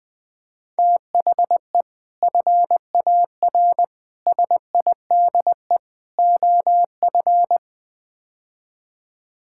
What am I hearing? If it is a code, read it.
Morse code, heard as a keyed tone, about 20 wpm, 700 Hz. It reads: THE FAR SIDE OF
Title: Build A Phrase - 5-Word Phrases - 20wpm